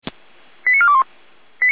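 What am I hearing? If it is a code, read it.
Five-tone selective call (ZVEI), heard as quick runs of five tones, tones of about 70 ms. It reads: 89421
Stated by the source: Five-tone ZVEI selective call received by SDR.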